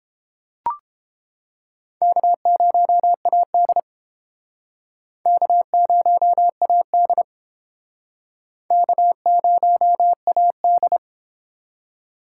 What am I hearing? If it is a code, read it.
Morse code, heard as a keyed tone, about 30 wpm, 700 Hz. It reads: K0AD K0AD K0AD